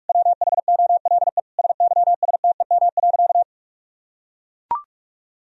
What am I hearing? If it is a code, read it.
Morse code, heard as a keyed tone, about 41 wpm, 700 Hz. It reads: WHOLE SYSTEM.
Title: Sentences from Top 1000 Words - Review Part 3 - 45wpm